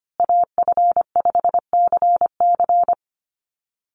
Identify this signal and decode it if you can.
Morse code, heard as a keyed tone, about 25 wpm, 700 Hz. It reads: AF5CC
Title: Call Signs Repeated 3x - 25wpm